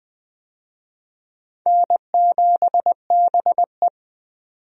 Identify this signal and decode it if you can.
Morse code, heard as a keyed tone, about 20 wpm, 700 Hz. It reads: N7BE
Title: USA Call Signs - 4 Characters - 20wpm